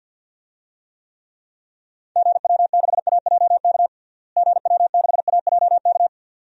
Morse code, heard as a keyed tone, about 48 wpm, 700 Hz. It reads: GW6AJK GW6AJK
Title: DXCC-100 Call Signs - 6 Characters - Repeated 2x - 50wpm